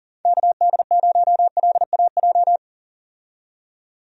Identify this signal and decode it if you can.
Morse code, heard as a keyed tone, about 40 wpm, 700 Hz. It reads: KD0LAJ
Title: USA Call Signs - 6 Characters - 40wpm